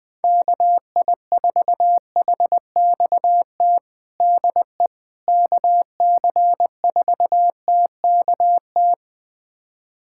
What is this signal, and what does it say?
Morse code, heard as a keyed tone, about 21 wpm, 700 Hz. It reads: KI4HXT DE KC4TKT
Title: USA Callsign DE Callsign - 6 Characters - 20wpm